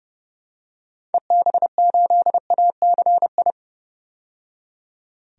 Morse code, heard as a keyed tone, about 31 wpm, 700 Hz. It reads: EB8ACI
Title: DXCC-100 Call Signs - 6 Characters - 30wpm